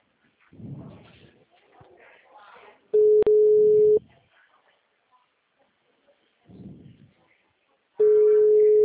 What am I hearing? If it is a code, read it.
Ringback tone (Europe, 425 Hz), with speech.
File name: out- -2006- - - .52935